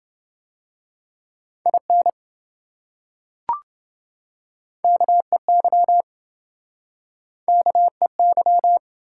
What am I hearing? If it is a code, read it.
Morse code, heard as a keyed tone, about 26 wpm, 700 Hz. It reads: IN KEY KEY